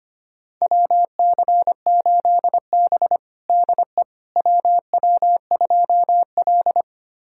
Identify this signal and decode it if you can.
Morse code, heard as a keyed tone, about 25 wpm, 700 Hz. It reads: WC8B DE WW2L
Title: USA Callsign DE Callsign - 4 Characters - 25wpm